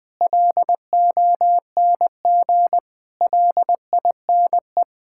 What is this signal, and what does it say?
Morse code, heard as a keyed tone, about 21 wpm, 700 Hz. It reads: LONG LINE
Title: Sets of 8 Words - 20wpm